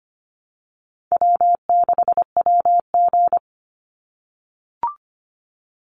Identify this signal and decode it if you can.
Morse code, heard as a keyed tone, about 25 wpm, 700 Hz. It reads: W6WG
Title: Easy Call Signs - 25wpm